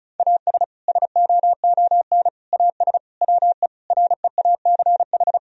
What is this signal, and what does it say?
Morse code, heard as a keyed tone, about 34 wpm, 700 Hz. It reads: AS SOON AS WE REACH
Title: Sentences from Top 1000 Words - Review Part 2 - 35wpm